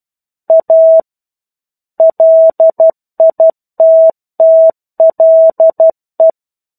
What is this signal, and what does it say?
Morse code, heard as a keyed tone, about 12 wpm, 650 Hz. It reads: A LITTLE
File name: 01-Chapter-01 12 WPM